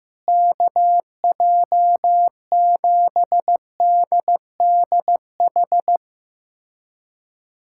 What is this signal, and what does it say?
Morse code, heard as a keyed tone, about 15 wpm, 700 Hz. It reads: KJ7DDH
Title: USA Callsign DE Callsign - 6 Characters - 15wpm